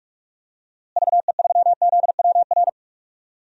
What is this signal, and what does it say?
Morse code, heard as a keyed tone, about 42 wpm, 700 Hz. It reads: VE3ZWR
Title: DXCC-100 Call Signs - 6 Characters - Repeated 3x - 45wpm